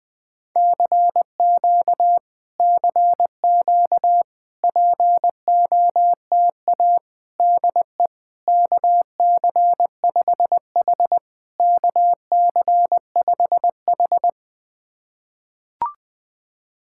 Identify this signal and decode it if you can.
Morse code, heard as a keyed tone, about 20 wpm, 700 Hz. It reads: CQ CQ POTA DE KC5H KC5H
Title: POTA QSO Practice - 20wpm